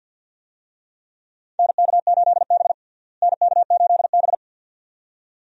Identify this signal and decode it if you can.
Morse code, heard as a keyed tone, about 50 wpm, 700 Hz. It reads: NK8B NK8B
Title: USA Call Signs - 4 Characters - Repeated 2x - 50wpm